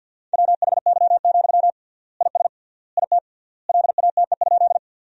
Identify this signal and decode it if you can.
Morse code, heard as a keyed tone, about 50 wpm, 700 Hz. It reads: WHY, IS IT LATE?
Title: 3x Word-Spacing - Sentences from Top 300 Words - 50wpm